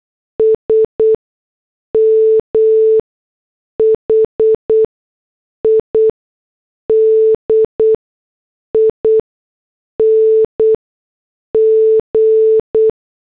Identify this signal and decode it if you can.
Morse code, heard as a keyed tone, about 8 wpm, spaced as for 6 wpm, 440 Hz. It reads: SM HIDING